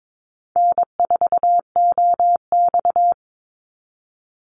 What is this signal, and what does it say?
Morse code, heard as a keyed tone, about 22 wpm, 700 Hz. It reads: N4OX
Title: Easy Call Signs - 22wpm